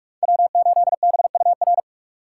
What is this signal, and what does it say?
Morse code, heard as a keyed tone, about 45 wpm, 700 Hz. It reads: W8BUR
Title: USA Call Signs - 5 Characters - 2x Character Spacing - 45wpm